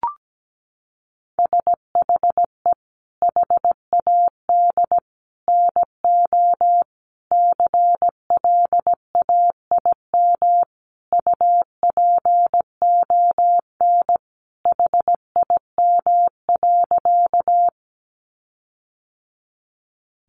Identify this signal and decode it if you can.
Morse code, heard as a keyed tone, about 17 wpm, 700 Hz. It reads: SHE HAD NO CLAIM UPON HIM.
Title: Sentences from Top 1500 Words 17wpm